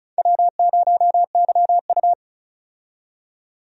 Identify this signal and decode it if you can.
Morse code, heard as a keyed tone, about 34 wpm, 700 Hz. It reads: W0YU
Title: USA Call Signs - 4 Characters - 35wpm